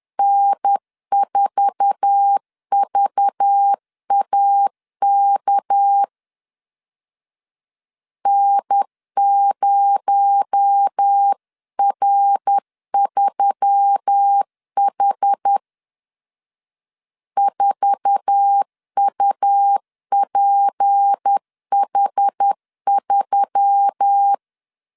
Morse code, heard as a keyed tone, about 10 wpm, 800 Hz. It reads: N4VAK N0R3H 4UPH3
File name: LEZIONI / 8 YJ45 / 8 YJ4 10wpm 2633